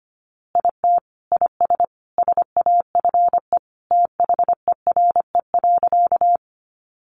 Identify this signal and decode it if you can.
Morse code, heard as a keyed tone, about 25 wpm, 700 Hz. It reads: IT IS SAFE THERE.
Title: Easy Phrases 25wpm